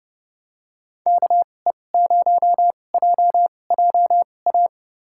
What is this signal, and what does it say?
Morse code, heard as a keyed tone, about 30 wpm, spaced as for 22 wpm, 700 Hz. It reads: KE0JJA
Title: USA Call Signs - 6 Characters - 2x Character Spacing - 30wpm